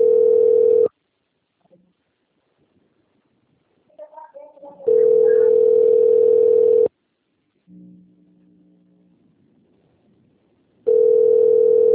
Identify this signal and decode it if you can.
Ringback tone (North America), 440 + 480 Hz.